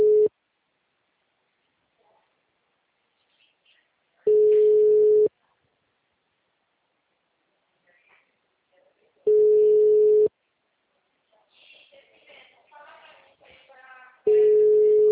Ringback tone (Europe, 425 Hz).
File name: out- -2010- - - .93